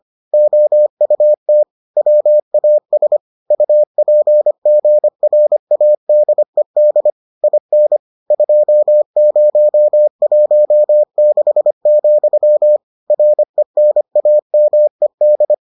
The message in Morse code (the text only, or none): OUT WAS UPGRADED IN 2016, RENAMED